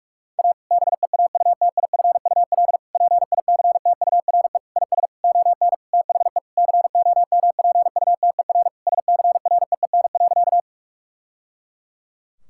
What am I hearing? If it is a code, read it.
Morse code, heard as a keyed tone, about 42 wpm, 700 Hz. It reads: A BEAUTIFUL PICTURE IS ON THE COMPUTER SCREEN.
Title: Sentences from Top 1000 Words - Review Part 6 - 45wpm